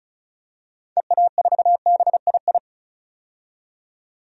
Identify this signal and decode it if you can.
Morse code, heard as a keyed tone, about 36 wpm, 700 Hz. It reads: EA4BII
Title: DXCC-100 Call Signs - 6 Characters - 35wpm